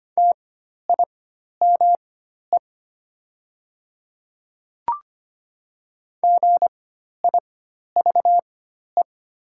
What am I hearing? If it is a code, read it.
Morse code, heard as a keyed tone, about 25 wpm, spaced as for 12 wpm, 700 Hz. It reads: TIME GIVE
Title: 4x Character Spacing - Top 200 Words - 25wpm